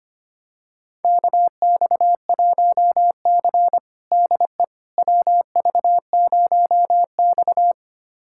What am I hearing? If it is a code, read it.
Morse code, heard as a keyed tone, about 25 wpm, 700 Hz. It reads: KX1C DE WV0X
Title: USA Callsign DE Callsign - 4 Characters - 25wpm